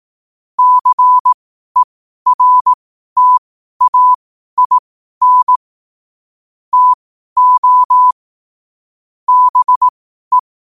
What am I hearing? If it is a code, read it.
Morse code, heard as a keyed tone, about 18 wpm, spaced as for 12 wpm, 1000 Hz. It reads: CERTAIN TO BE
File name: qotdfiles / 12wpm